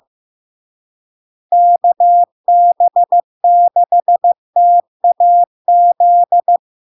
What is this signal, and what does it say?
Morse code, heard as a keyed tone, about 15 wpm, 700 Hz. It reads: KB6TAZ
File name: LARK-FD-2022-Log-in-CW-15wpm-700-Hz